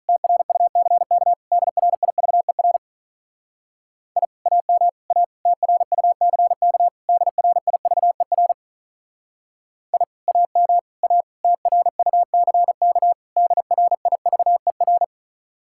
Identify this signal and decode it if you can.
Morse code, heard as a keyed tone, about 44 wpm, 700 Hz. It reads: TRUCK DRIVER I AM A TRUCK DRIVER I AM A TRUCK DRIVER